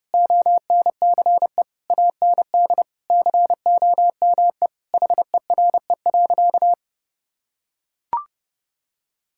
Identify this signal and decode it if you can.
Morse code, heard as a keyed tone, about 30 wpm, 700 Hz. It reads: ONCE AND COME HERE.
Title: Sentences from Top 1600 Words 30wpm